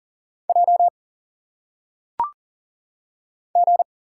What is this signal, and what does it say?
Morse code, heard as a keyed tone, about 40 wpm, 700 Hz. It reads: J G
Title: Single Letters - Phonetic Pronunciation 40wpm